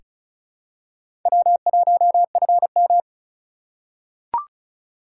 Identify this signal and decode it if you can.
Morse code, heard as a keyed tone, about 36 wpm, 700 Hz. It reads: W1FM
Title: USA Call Signs - 2x Character Spacing - 35wpm